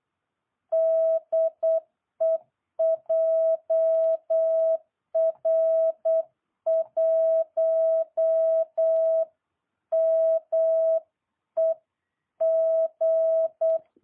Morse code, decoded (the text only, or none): DE JR1MEG